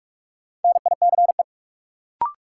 Morse code, heard as a keyed tone, about 46 wpm, 700 Hz. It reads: NICE
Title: Sentences from Top 1100 Words - New Word Drill Alt - 45wpm